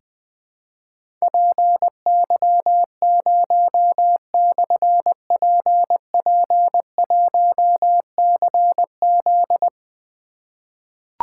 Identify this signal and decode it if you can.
Morse code, heard as a keyed tone, about 21 wpm, 700 Hz. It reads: PY0/PP1CZ
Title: Difficult Call Signs 20wpm